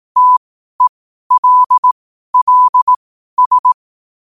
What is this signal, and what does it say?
Morse code, heard as a keyed tone, about 18 wpm, spaced as for 12 wpm, 1000 Hz. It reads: TELLS
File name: qotdfiles / 12wpm